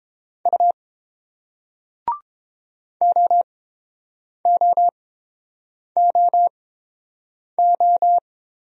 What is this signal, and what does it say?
Morse code, heard as a keyed tone, about 30 wpm, spaced as for 7 wpm, 700 Hz. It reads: U OOOO